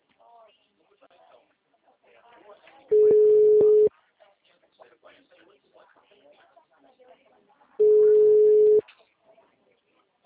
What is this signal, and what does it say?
Ringback tone (Europe, 425 Hz), with speech.
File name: out- -2002- - - .22359